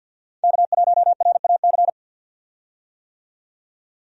Morse code, heard as a keyed tone, about 51 wpm, 700 Hz. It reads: K1RAC
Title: USA Call Signs - 5 Characters - 50wpm